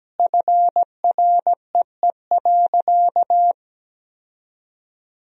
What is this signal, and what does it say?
Morse code, heard as a keyed tone, about 17 wpm, 700 Hz. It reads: FREE.
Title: 3x Word-Spacing - Sentences from Top 500 Words - 17wpm